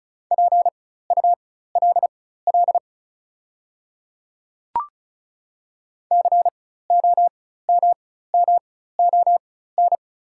Morse code, heard as a keyed tone, about 36 wpm, spaced as for 16 wpm, 700 Hz. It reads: PULL COMMON
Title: 4x Character Spacing - Top 500 Words - 35wpm